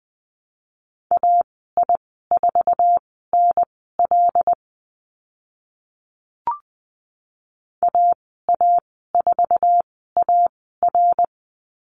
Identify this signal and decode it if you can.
Morse code, heard as a keyed tone, about 20 wpm, spaced as for 14 wpm, 700 Hz. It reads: AI4NL AA4AR
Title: 082 Lesson 14g - Number 2 - Call Signs 20wpm